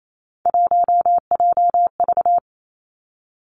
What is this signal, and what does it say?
Morse code, heard as a keyed tone, about 28 wpm, 700 Hz. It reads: 1JV
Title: Three Challenging Letter-Number Combinations 28wpm